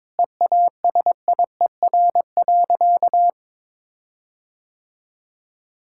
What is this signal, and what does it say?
Morse code, heard as a keyed tone, about 21 wpm, 700 Hz. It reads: EASIER.
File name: Difficult Sentences with Hints - 22wpm